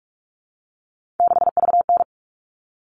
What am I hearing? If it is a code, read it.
Morse code, heard as a keyed tone, about 44 wpm, 700 Hz. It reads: 6VN
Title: All Cut Numbers - 3-Digit 45wpm